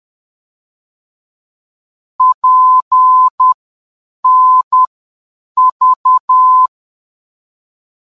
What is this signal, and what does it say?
Morse code, heard as a keyed tone, about 10 wpm, spaced as for 7 wpm, 1020 Hz. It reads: PNV
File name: PNV-VOR DME